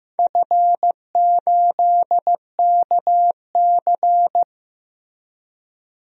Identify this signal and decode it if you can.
Morse code, heard as a keyed tone, about 15 wpm, 700 Hz. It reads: F8KC
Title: DXCC-100 Call Signs - 4 Characters - Repeated 3x - 15wpm